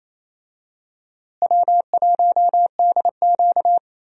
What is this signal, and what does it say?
Morse code, heard as a keyed tone, about 28 wpm, 700 Hz. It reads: W1DQ
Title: USA Call Signs - 4 Characters - Repeated 2x - 28wpm